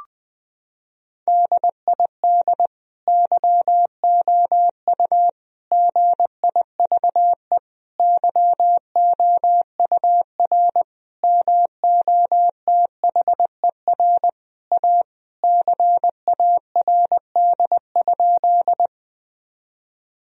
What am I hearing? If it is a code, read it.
Morse code, heard as a keyed tone, about 20 wpm, 700 Hz. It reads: DID YOU GIVE YOUR MOTHER A CARD?
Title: Sentences from Top 1000 Words - Review Part 4 - 20wpm